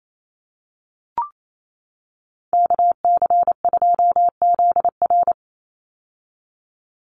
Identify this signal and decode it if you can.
Morse code, heard as a keyed tone, about 28 wpm, 700 Hz. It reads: KC2ZR
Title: USA Call Signs 28wpm